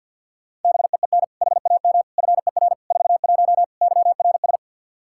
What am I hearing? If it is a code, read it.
Morse code, heard as a keyed tone, about 52 wpm, 700 Hz. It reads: BEEN HAM FER 41 YRS
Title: K1USN SST QSO Practice 50wpm